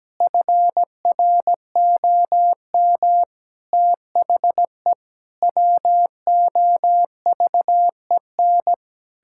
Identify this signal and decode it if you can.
Morse code, heard as a keyed tone, about 17 wpm, 700 Hz. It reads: FROM THE WOVEN
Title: Pangram Challenge v2 - 17wpm